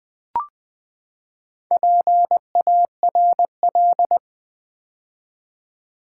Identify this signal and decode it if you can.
Morse code, heard as a keyed tone, about 20 wpm, 700 Hz. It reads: PARL
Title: Build A Word - 4 Letters At A Time - 10-Letter Words - 20wpm